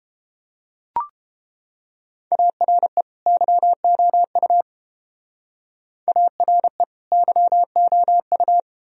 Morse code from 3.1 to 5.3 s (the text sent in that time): YOU